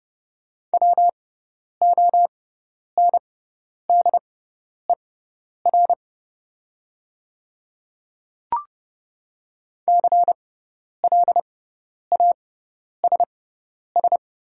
Morse code, heard as a keyed tone, about 31 wpm, spaced as for 10 wpm, 690 Hz. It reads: WONDER CLASS
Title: 6x Character Spacing - Top 500 Words - 30wpm